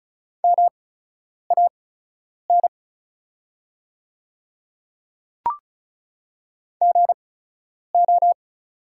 Morse code, heard as a keyed tone, about 36 wpm, spaced as for 10 wpm, 700 Hz. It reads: MAN GO
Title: 8x Character Spacing - Top 200 Words - 35wpm